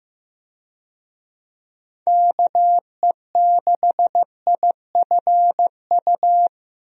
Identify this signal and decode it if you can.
Morse code, heard as a keyed tone, about 15 wpm, 700 Hz. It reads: KE6IFU
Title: USA Call Signs - 6 Characters - Repeated 2x - 15wpm